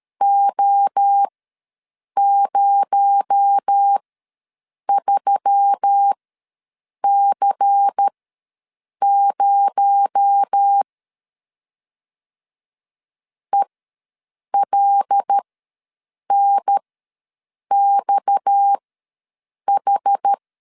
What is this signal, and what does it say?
Morse code, heard as a keyed tone, about 13 wpm, spaced as for 7 wpm, 800 Hz. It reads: O03C0 ELNXH
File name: LEZIONI / 6 PKX0 / 6 PKX0 ws 12wpm 3448